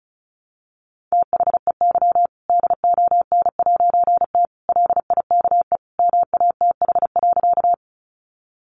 Morse code, heard as a keyed tone, about 34 wpm, 700 Hz. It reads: THEY DON'T LIKE MATH.
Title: Easy Phrases 35wpm